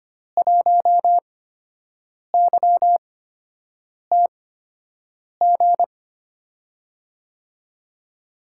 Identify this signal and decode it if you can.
Morse code, heard as a keyed tone, about 25 wpm, spaced as for 7 wpm, 700 Hz. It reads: 1YTG
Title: USA Call Signs - 5 Characters - 8x Character Spacing - 25wpm